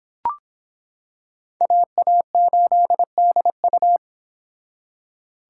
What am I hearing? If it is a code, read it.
Morse code, heard as a keyed tone, about 26 wpm, 700 Hz. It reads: AA8DU